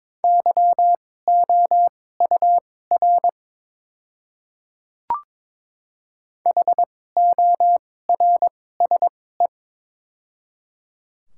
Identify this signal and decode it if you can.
Morse code, heard as a keyed tone, about 22 wpm, spaced as for 16 wpm, 700 Hz. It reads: YOUR HORSE
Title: 2x Character Spacing - Top 300 Words - 22wpm